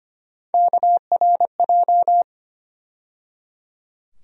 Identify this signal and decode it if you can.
Morse code, heard as a keyed tone, about 25 wpm, 700 Hz. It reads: KRJ
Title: USA Call Sign Postfixes - 3 Letters 25wpm